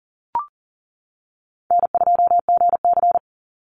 Morse code, heard as a keyed tone, about 40 wpm, 700 Hz. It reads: N2GC
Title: Easy Call Signs - 40wpm